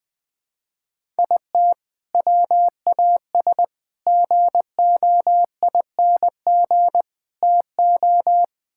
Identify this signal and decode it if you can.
Morse code, heard as a keyed tone, about 20 wpm, 700 Hz. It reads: IT WAS GOING TO
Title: Build A Phrase - 4-Word Phrases - 20wpm